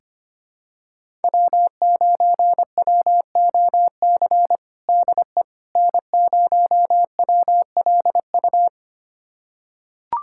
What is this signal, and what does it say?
Morse code, heard as a keyed tone, about 25 wpm, 700 Hz. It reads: W9WOC DE N0WLU
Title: USA Callsign DE Callsign - 5 Characters - 25wpm